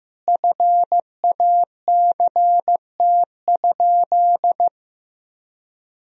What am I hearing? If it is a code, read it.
Morse code, heard as a keyed tone, about 16 wpm, 700 Hz. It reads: FACT?
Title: Sentences from Top 1000 Words - Review Part 4 - 15wpm